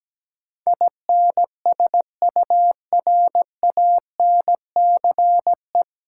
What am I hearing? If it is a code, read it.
Morse code, heard as a keyed tone, about 17 wpm, 700 Hz. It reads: INSURANCE
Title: Sentences from Top 1100 Words - New Word Drill - 17wpm